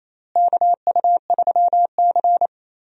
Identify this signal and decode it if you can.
Morse code, heard as a keyed tone, about 28 wpm, 700 Hz. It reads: KU3C
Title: USA Call Signs - 4 Characters - Repeated 3x - 28wpm